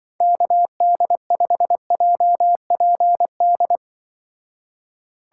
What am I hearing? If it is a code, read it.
Morse code, heard as a keyed tone, about 25 wpm, 700 Hz. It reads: KD5JPD
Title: Speed Racing - USA 2x3 Call Signs 20wpm